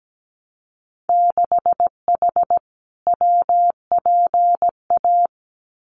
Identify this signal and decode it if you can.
Morse code, heard as a keyed tone, about 17 wpm, 700 Hz. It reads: 6H WPA
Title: Winter Field Day Exchanges Encore 17wpm